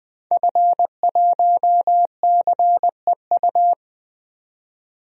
Morse code, heard as a keyed tone, about 20 wpm, 700 Hz. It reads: F1CEU
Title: DXCC-100 Call Signs - 5 Characters - 20wpm